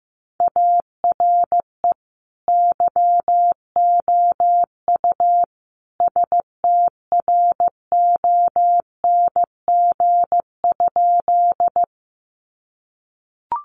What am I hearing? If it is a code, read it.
Morse code, heard as a keyed tone, about 15 wpm, 700 Hz. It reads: ARE YOU STRONG?
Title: Sentences from Top 500 Words 15wpm